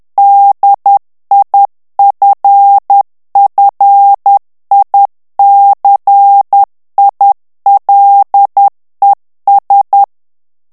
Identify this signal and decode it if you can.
Morse code, heard as a keyed tone, about 10 wpm, 800 Hz. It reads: DIFFICILES
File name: FRANCAIS TEXTE CLAIR 2 VITESSE 10 MOTS MINUTES